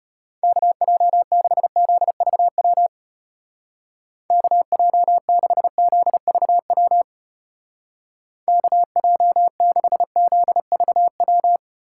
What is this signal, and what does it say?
Morse code, heard as a keyed tone, about 32 wpm, 700 Hz. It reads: KJ6ZVW KJ6ZVW KJ6ZVW